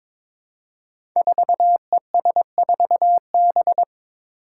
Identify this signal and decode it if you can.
Morse code, heard as a keyed tone, about 21 wpm, 700 Hz. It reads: 4ES4B
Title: Five Challenging Letter-Number Combinations 22wpm